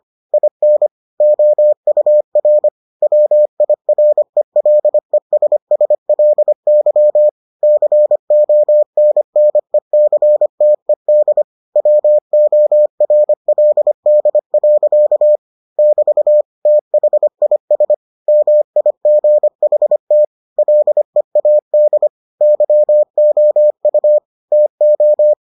Morse code, heard as a keyed tone, about 25 wpm, 600 Hz. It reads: IN OUR WIRELESSLY CONNECTED WORLD. <BT> THIS MIGHT LEAD YOU TO